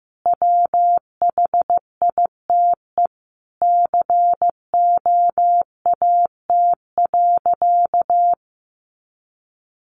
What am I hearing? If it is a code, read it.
Morse code, heard as a keyed tone, about 15 wpm, 700 Hz. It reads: WHITE COAT.
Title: Additional Easy Phrases 15wpm